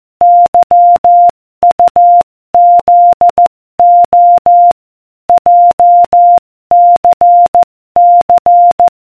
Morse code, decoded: YUZO JCC